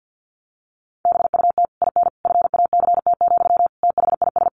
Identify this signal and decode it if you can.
Morse code, heard as a keyed tone, about 50 wpm, 700 Hz. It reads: BUT IN FACT, THIS
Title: Sentences from Top 1500 Words 50wpm